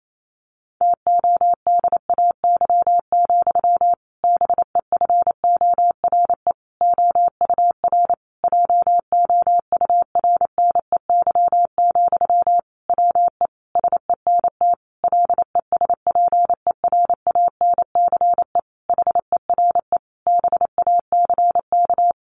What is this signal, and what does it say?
Morse code, heard as a keyed tone, about 28 wpm, 700 Hz. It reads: TODAY, BEFORE OUR JOURNEY, WE SENT LESPERANCE HERE BACK
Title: A Sound Of Thunder 28wpm